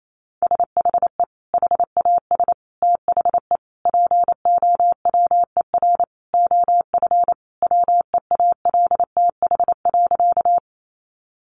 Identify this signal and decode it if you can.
Morse code, heard as a keyed tone, about 28 wpm, 700 Hz. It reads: SHE HAS THE POWER OF WEALTH.
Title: Sentences from Top 1200 Words 28wpm